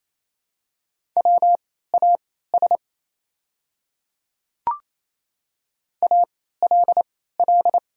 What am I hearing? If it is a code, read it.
Morse code, heard as a keyed tone, about 28 wpm, spaced as for 16 wpm, 700 Hz. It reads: WAS ALL
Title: 3x Character Spacing - Top 300 Words - 28wpm